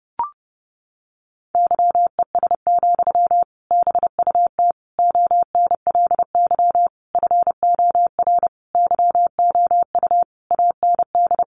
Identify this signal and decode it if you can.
Morse code, heard as a keyed tone, about 30 wpm, 700 Hz. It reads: YES, BUT ONLY FOR YOU AND
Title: Sentences from Top 500 Words 30wpm